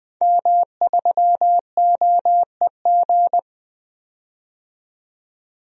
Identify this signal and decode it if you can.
Morse code, heard as a keyed tone, about 20 wpm, 700 Hz. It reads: M3OEG
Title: DXCC-100 Call Signs - 5 Characters - 20wpm